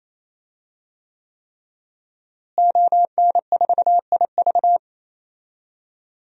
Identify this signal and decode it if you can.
Morse code, heard as a keyed tone, about 28 wpm, 700 Hz. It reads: ON4IV
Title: DXCC-100 Call Signs - 5 Characters - Repeated 3x - 28wpm